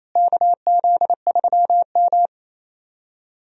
Morse code, heard as a keyed tone, about 27 wpm, 700 Hz. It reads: KZ3M
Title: USA Call Signs - 4 Characters - Repeated 2x - 28wpm